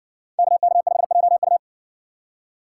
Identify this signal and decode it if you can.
Morse code, heard as a keyed tone, about 57 wpm, 700 Hz. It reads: KK5JU